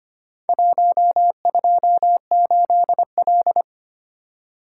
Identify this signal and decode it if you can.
Morse code, heard as a keyed tone, about 25 wpm, 700 Hz. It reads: 128L
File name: practice / 25wpm / Four Letter-Number Combinations - 25wpm